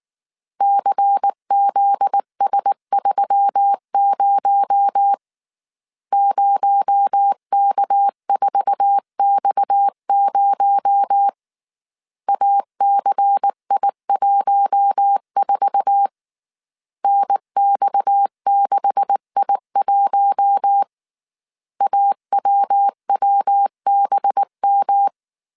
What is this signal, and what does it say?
Morse code, heard as a keyed tone, about 19 wpm, 800 Hz. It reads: CZS30 0K4X0 ACI14 NX6I1 AWWBM
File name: CW / DET / 1 9 1500C S A 18wpm 2459